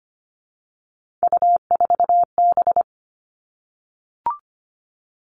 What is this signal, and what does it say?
Morse code, heard as a keyed tone, about 25 wpm, 700 Hz. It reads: U4B
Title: Three Challenging Letter-Number Combinations 25wpm